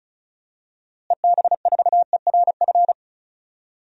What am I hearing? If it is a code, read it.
Morse code, heard as a keyed tone, about 36 wpm, 700 Hz. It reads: EB4ERF
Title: DXCC-100 Call Signs - 6 Characters - Repeated 2x - 35wpm